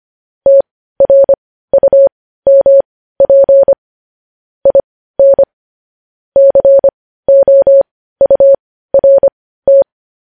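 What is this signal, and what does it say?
Morse code, heard as a keyed tone, about 25 wpm, spaced as for 15 wpm, 550 Hz. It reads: TRUMP IN COURT
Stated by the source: Morse code practice of news headlines at 15 wpm, opening with vvv.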